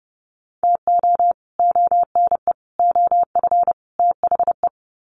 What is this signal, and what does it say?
Morse code, heard as a keyed tone, about 30 wpm, 700 Hz. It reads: TO ONE OF THE
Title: Sets of 4 Words 30wpm